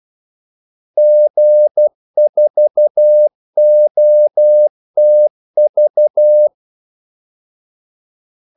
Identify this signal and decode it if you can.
Morse code, heard as a keyed tone, about 12 wpm, 600 Hz. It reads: G4OTV
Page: 2 ekstra ordmellemrum mellem kaldesignalerne